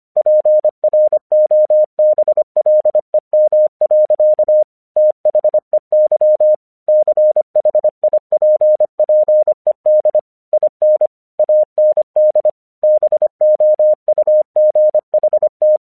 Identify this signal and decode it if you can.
Morse code, heard as a keyed tone, about 25 wpm, 610 Hz. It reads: PROBLEM. THEY CHIPPED IN AND BOUGHT
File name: 1 English 25wpm / AQC14